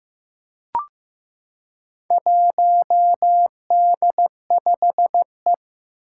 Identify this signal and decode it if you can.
Morse code, heard as a keyed tone, about 15 wpm, 700 Hz. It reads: 1D5E
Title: Four Challenging Letter-Number Combinations 15wpm